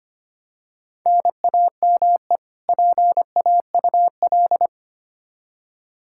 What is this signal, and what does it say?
Morse code, heard as a keyed tone, about 25 wpm, 700 Hz. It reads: NAME PAUL
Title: Practice QSO - Exchange 1 - Beginning - 25wpm